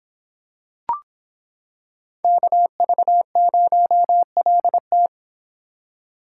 Morse code, heard as a keyed tone, about 26 wpm, 700 Hz. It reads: KV0LT